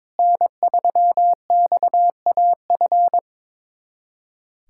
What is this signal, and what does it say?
Morse code, heard as a keyed tone, about 22 wpm, 700 Hz. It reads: N3XAF
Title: USA Callsign DE Callsign 22wpm